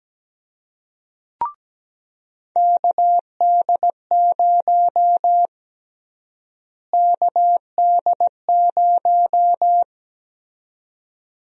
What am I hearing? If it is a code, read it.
Morse code, heard as a keyed tone, about 17 wpm, 700 Hz. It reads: KD0 KD0
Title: USA Call Sign Prefixes - Repeated 2x - 17wpm